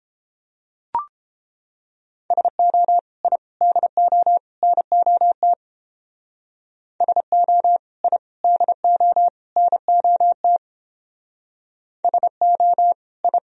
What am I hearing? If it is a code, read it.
Morse code, heard as a keyed tone, about 30 wpm, 700 Hz. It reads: SO I DO NOT SO I DO NOT SO I